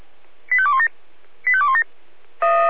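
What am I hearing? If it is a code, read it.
Five-tone selective call (ZVEI), heard as quick runs of five tones, tones of about 70 ms. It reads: 85217 85217
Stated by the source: ZVEI five-tone selective call 85216 received on SDR.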